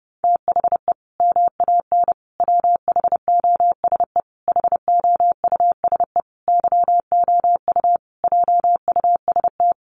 Morse code, heard as a keyed tone, about 30 wpm, 700 Hz. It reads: THE MAN WHOSE HOUSE YOU JUST
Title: Sentences from Top 1200 Words 30wpm